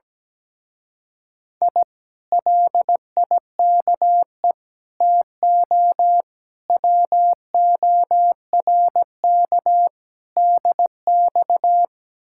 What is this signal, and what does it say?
Morse code, heard as a keyed tone, about 17 wpm, 700 Hz. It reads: I LIKE TO WORK DX
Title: Speed-Racing Common QSO Phrases - 17wpm